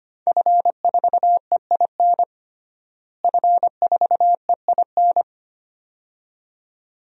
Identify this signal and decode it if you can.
Morse code, heard as a keyed tone, about 25 wpm, 700 Hz. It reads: F4EIN F4EIN
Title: DXCC-100 Call Signs - 5 Characters - Repeated 2x - 25wpm